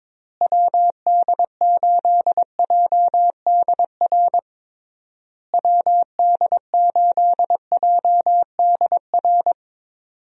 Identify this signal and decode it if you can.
Morse code, heard as a keyed tone, about 22 wpm, 700 Hz. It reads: WD8JDR WD8JDR
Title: USA Call Signs - 6 Characters - Repeated 3x - 22wpm